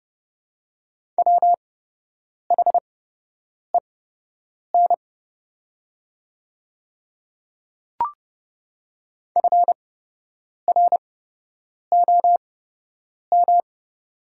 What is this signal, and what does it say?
Morse code, heard as a keyed tone, about 30 wpm, spaced as for 8 wpm, 700 Hz. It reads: WHEN FROM
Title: 8x Character Spacing - Top 100 Words - 30wpm